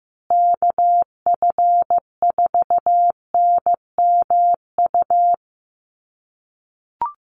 Morse code, heard as a keyed tone, about 15 wpm, 700 Hz. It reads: KF4NMU
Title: USA Call Signs Repeated 3x 15wpm